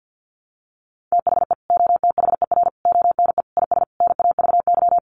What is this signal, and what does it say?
Morse code, heard as a keyed tone, about 50 wpm, 700 Hz. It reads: THE OTHER ONE IS NAVY
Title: Additional Easy Phrases 50wpm